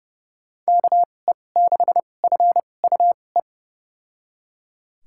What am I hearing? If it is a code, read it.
Morse code, heard as a keyed tone, about 31 wpm, spaced as for 22 wpm, 700 Hz. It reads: KE6FUE
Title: USA Call Signs - 6 Characters - 2x Character Spacing - 30wpm